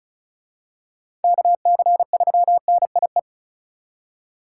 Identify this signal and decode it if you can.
Morse code, heard as a keyed tone, about 34 wpm, 700 Hz. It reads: KC3NIE
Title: K1USN SST QSO Practice - 35wpm